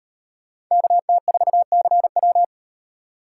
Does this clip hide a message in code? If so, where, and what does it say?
Yes: Morse code from 0.7 to 2.4 s, heard as a keyed tone, about 38 wpm, 700 Hz. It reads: KT4CW